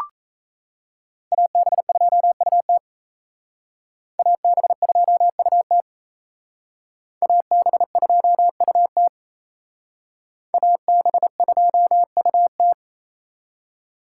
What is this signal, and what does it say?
Morse code, heard as a keyed tone, about 35 wpm, 700 Hz. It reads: AB2UT AB2UT AB2UT AB2UT
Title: Speed Racing - USA 2x2 Call Signs 28wpm